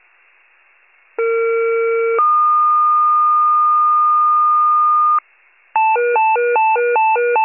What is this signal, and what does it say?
Two-tone page, heard as a one-second tone, then a three-second tone, about 460 then 1220 Hz. It is heard from 1.2 to 5.2 s.